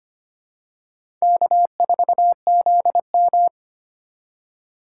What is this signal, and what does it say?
Morse code, heard as a keyed tone, about 25 wpm, 700 Hz. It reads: K4ZM
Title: USA Call Signs - 3x Character Spacing - 25wpm